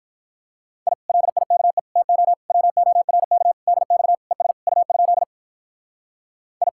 Morse code, heard as a keyed tone, about 47 wpm, 700 Hz. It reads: I LIKE TO WORK DX ES U? I